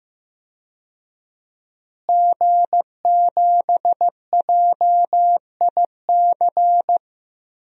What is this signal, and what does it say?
Morse code, heard as a keyed tone, about 16 wpm, 700 Hz. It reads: G7JIC
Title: DXCC-100 Call Signs - 5 Characters - Repeated 2x - 15wpm